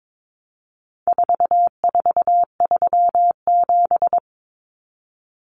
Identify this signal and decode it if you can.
Morse code, heard as a keyed tone, about 22 wpm, 700 Hz. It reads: 4437
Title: Numbers - 4-Digit 22wpm